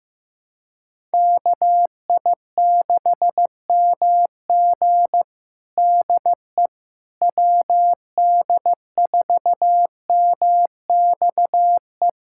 Morse code, heard as a keyed tone, about 15 wpm, 700 Hz. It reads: KI6MG DE WD4MXE
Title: USA Callsign DE Callsign 15wpm